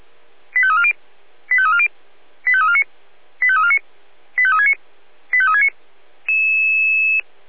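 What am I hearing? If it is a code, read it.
Five-tone selective call (ZVEI), heard as quick runs of five tones, tones of about 70 ms. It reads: 85340 85340 85349 85349 85369 85369
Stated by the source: ZVEI five-tone selective call 85355 received on an SDR.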